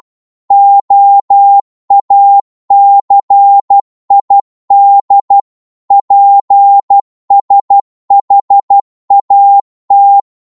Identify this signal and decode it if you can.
Morse code, heard as a keyed tone, about 12 wpm, 800 Hz. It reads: OACID PSHAT
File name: Morse / ltr-12wpm